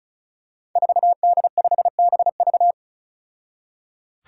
Morse code, heard as a keyed tone, about 36 wpm, 700 Hz. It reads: 4D5BV
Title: Five Challenging Letter-Number Combinations 35wpm